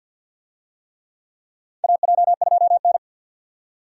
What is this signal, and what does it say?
Morse code, heard as a keyed tone, about 47 wpm, 700 Hz. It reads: AJ2N
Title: USA Call Signs - 4 Characters - 50wpm